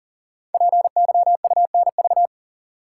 Morse code, heard as a keyed tone, about 42 wpm, 700 Hz. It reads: PYUNV
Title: Code Groups - 5 Letters - 40wpm